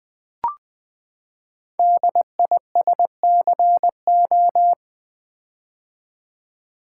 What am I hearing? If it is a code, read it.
Morse code, heard as a keyed tone, about 20 wpm, 700 Hz. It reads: DISCO
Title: Build A Word - 5 Letters At A Time - 10-Letter Words - 20wpm